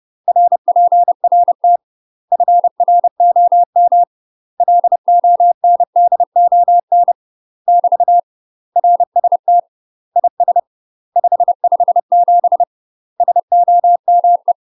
Morse code, heard as a keyed tone, about 30 wpm, 700 Hz. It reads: RPRT FROM LONDON <BT> RST IS 557 SOME
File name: UK MASTER QSOs TRAINING / 30WPM / UK MASTER QSOs 0038